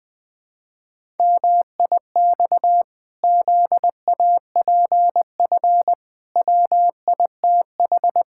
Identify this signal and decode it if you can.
Morse code, heard as a keyed tone, about 21 wpm, 700 Hz. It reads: MIX ZAPF WITH
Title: Pangram Challenge v2 - 20wpm